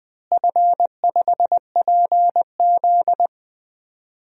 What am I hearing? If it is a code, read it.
Morse code, heard as a keyed tone, about 19 wpm, 700 Hz. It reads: F5PZ
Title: DXCC-100 Call Signs - 4 Characters - Repeated 3x - 20wpm